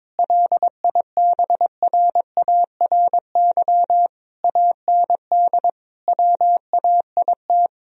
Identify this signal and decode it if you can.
Morse code, heard as a keyed tone, about 22 wpm, 700 Hz. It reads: LIBRARY AND WAIT
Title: Sentences from Top 1000 Words - Review Part 4 - 22wpm